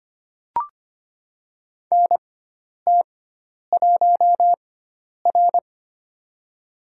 Morse code, heard as a keyed tone, about 25 wpm, spaced as for 10 wpm, 700 Hz. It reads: NT1R
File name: Lesson 16g - Letter c - Call Signs 25wpm